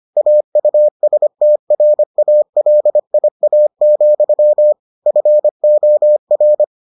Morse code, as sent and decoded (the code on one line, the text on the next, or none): .- ..- ... - .-. .- .-.. .. .- --..-- / ..-. --- .-.
AUSTRALIA, FOR